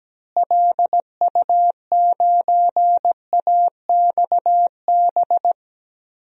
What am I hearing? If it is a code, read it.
Morse code, heard as a keyed tone, about 17 wpm, 700 Hz. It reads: LU9AXB
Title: DXCC-100 Call Signs - 6 Characters - Repeated 2x - 17wpm